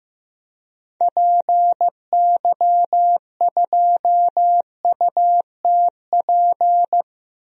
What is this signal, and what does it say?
Morse code, heard as a keyed tone, about 15 wpm, 690 Hz. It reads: PY2UTP
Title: DXCC-100 Call Signs - 6 Characters - 15wpm